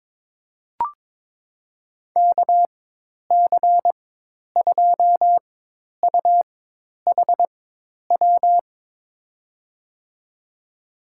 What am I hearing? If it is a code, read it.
Morse code, heard as a keyed tone, about 22 wpm, spaced as for 10 wpm, 700 Hz. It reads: KC2UHW
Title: USA Call Signs - 4x Character Spacing - 22wpm